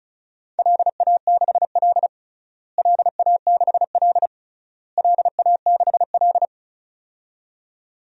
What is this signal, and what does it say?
Morse code, heard as a keyed tone, about 36 wpm, 700 Hz. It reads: LA6L LA6L LA6L
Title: DXCC-100 Call Signs - 4 Characters - Repeated 3x - 35wpm